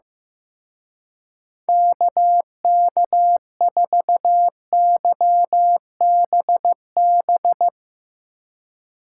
Morse code, heard as a keyed tone, about 15 wpm, 700 Hz. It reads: KK4YBB
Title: Speed Racing - USA 2x3 Call Signs 15wpm